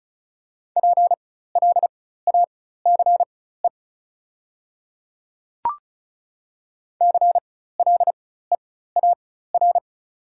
Morse code, heard as a keyed tone, about 35 wpm, spaced as for 16 wpm, 690 Hz. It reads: PLACE CLEAR
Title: 4x Character Spacing - Top 500 Words - 35wpm